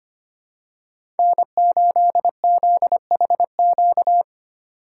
Morse code, heard as a keyed tone, about 25 wpm, 700 Hz. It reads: N8ZHQ
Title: USA Call Signs - 5 Characters - 3x Character Spacing - 25wpm